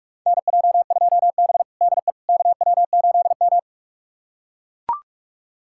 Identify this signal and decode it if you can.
Morse code, heard as a keyed tone, about 44 wpm, 700 Hz. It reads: NJ2B DE KW8M
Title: USA Callsign DE Callsign - 4 Characters - 45wpm